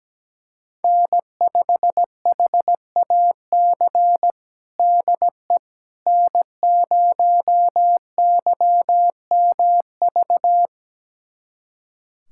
Morse code, heard as a keyed tone, about 17 wpm, 700 Hz. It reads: N5HAC DE N0YMV
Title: USA Callsign DE Callsign - 5 Characters - 17wpm